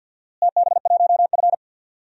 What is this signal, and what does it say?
Morse code, heard as a keyed tone, about 45 wpm, 700 Hz. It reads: TB1F
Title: DXCC-100 Call Signs - 4 Characters - Repeated 2x - 50wpm